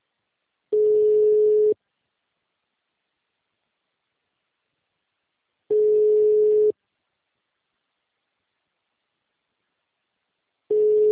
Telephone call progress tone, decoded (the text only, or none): ringback tone